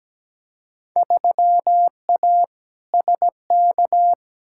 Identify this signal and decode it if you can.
Morse code, heard as a keyed tone, about 17 wpm, 700 Hz. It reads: 3A SK
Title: Field Day Exchanges Encore - 17wpm